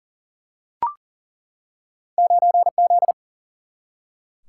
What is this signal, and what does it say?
Morse code, heard as a keyed tone, about 40 wpm, 700 Hz. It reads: 9Z
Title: Single-Letter Single-Number 40wpm